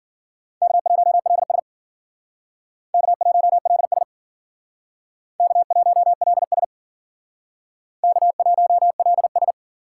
Morse code, heard as a keyed tone, about 51 wpm, 700 Hz. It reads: K1LS K1LS K1LS K1LS